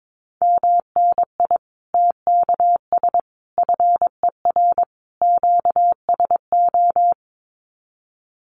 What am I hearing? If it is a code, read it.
Morse code, heard as a keyed tone, about 22 wpm, 700 Hz. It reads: MNI TKS FER QSO
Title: Common QSO Phrases 22wpm